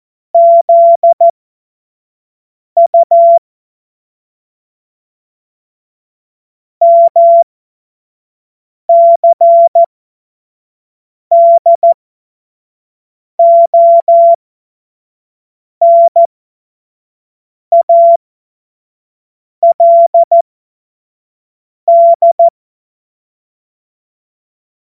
Morse code, heard as a keyed tone, about 14 wpm, spaced as for 5 wpm, 680 Hz. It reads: ZU MCDONALD
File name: Morsekurs / Witze / Witz 18. Mai 2021, 25 F70